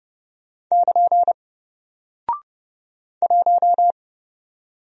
Morse code, heard as a keyed tone, about 31 wpm, 700 Hz. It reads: <KN> 1
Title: Single Character (Letter:Number:Punctuation:Prosign) Alt 30wpm